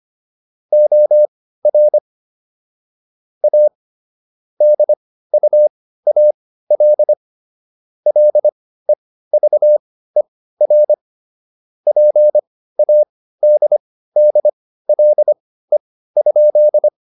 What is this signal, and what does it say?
Morse code, heard as a keyed tone, about 25 wpm, spaced as for 15 wpm, 600 Hz. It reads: OR A DUAL LEVER PADDLE?